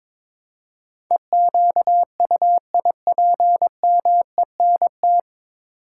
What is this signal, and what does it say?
Morse code, heard as a keyed tone, about 22 wpm, 700 Hz. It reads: EQUIPMENT
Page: Morse Code Ninja practice 22wpm